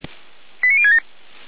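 Five-tone selective call (ZVEI), heard as quick runs of five tones, tones of about 70 ms. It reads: 89067
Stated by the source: ZVEI five-tone selective call received by SDR.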